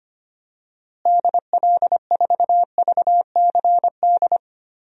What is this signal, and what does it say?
Morse code, heard as a keyed tone, about 25 wpm, 700 Hz. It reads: DL4VCD
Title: DXCC-100 Call Signs - 6 Characters - Repeated 2x - 25wpm